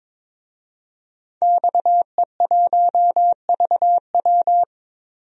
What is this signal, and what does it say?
Morse code, heard as a keyed tone, about 22 wpm, 700 Hz. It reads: XE1VW
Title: DXCC-100 Call Signs - 5 Characters - 22wpm